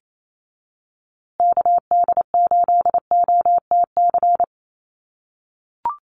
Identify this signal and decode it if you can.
Morse code, heard as a keyed tone, about 28 wpm, 700 Hz. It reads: KD8OTC
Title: USA Call Signs Repeated 3x 28wpm